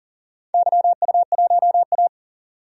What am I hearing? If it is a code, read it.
Morse code, heard as a keyed tone, about 37 wpm, 700 Hz. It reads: YU1A
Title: DXCC-100 Call Signs - 4 Characters - 40wpm